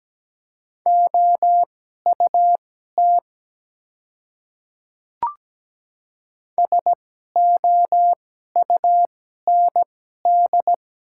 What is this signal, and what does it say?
Morse code, heard as a keyed tone, about 17 wpm, spaced as for 12 wpm, 700 Hz. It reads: OUT SOUND
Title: 2x Character Spacing - Top 100 Words - 17wpm